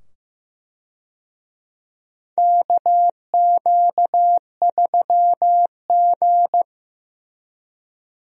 Morse code, heard as a keyed tone, about 16 wpm, 700 Hz. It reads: KQ3G
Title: USA Call Signs - 4 Characters - 15wpm